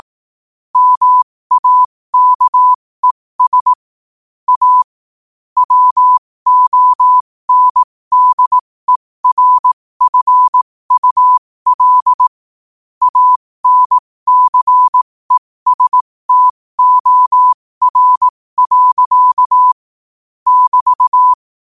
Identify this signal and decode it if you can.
Morse code, heard as a keyed tone, about 18 wpm, spaced as for 15 wpm, 1000 Hz. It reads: MAKES A WONDERFUL ANCESTOR. <BT>
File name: qotdfiles / 15wpm